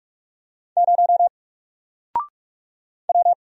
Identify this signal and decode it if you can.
Morse code, heard as a keyed tone, about 45 wpm, 700 Hz. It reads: 0 W
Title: Single Character (Letter:Number:Punctuation:Prosign) 45wpm